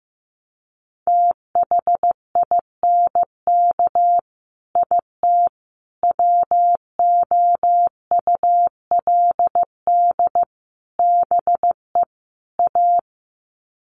Morse code, heard as a keyed tone, about 15 wpm, 700 Hz. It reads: THINK IT WOULD BE A
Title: Sets of 5 Words Encore2 15wpm